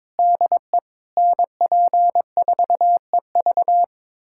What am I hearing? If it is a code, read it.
Morse code, heard as a keyed tone, about 22 wpm, 700 Hz. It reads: DE NP4EV
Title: USA Callsign DE Callsign 22wpm